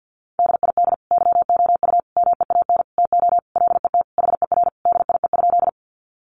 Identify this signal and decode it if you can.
Morse code, heard as a keyed tone, about 50 wpm, 700 Hz. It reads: DID YOU MEAN TO LET HER DIE?
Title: Sentences from Top 1300 Words 50wpm